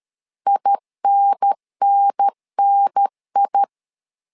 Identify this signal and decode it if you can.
Morse code, heard as a keyed tone, about 13 wpm, 800 Hz. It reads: INNNI
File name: LEZIONI / 3 MWRO / 3 MW 12wpm 1633